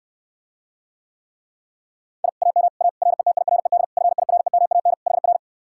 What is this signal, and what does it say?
Morse code, heard as a keyed tone, about 69 wpm, 700 Hz. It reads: I AM A RETIRED FIREMAN HR